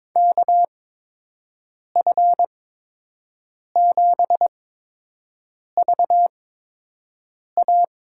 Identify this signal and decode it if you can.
Morse code, heard as a keyed tone, about 22 wpm, spaced as for 6 wpm, 700 Hz. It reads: KF7VA
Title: USA Call Signs - 6 Characters - 8x Character Spacing - 22wpm